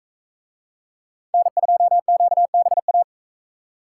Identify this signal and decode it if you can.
Morse code, heard as a keyed tone, about 42 wpm, 700 Hz. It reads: N2QBA